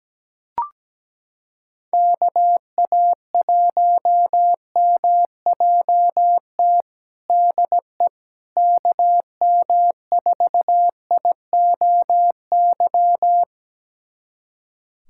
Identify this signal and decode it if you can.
Morse code, heard as a keyed tone, about 17 wpm, 700 Hz. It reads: KA1MJT DE KM4IOY
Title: USA Callsign DE Callsign - 6 Characters - 17wpm